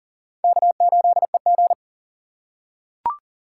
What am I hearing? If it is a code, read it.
Morse code, heard as a keyed tone, about 38 wpm, 700 Hz. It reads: K8EG
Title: USA Call Signs - 4 Characters - 3x Character Spacing - 40wpm